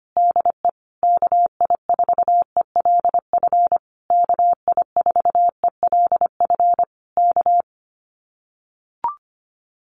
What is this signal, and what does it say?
Morse code, heard as a keyed tone, about 25 wpm, 700 Hz. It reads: DE KI4ELF KI4ELF K
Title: Common QSO Phrases Encore 25wpm